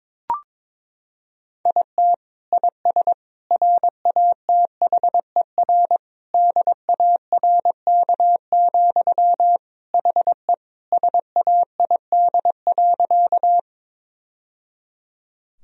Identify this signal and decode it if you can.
Morse code, heard as a keyed tone, about 22 wpm, 700 Hz. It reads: IT IS RATHER DARK, HE SAID.
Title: Sentences from Top 1000 Words - Review Part 4 - 22wpm